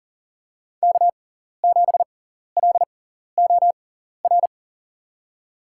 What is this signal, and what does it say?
Morse code, heard as a keyed tone, about 40 wpm, spaced as for 14 wpm, 700 Hz. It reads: K7LOR
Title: USA Call Signs - 5 Characters - 6x Character Spacing - 40wpm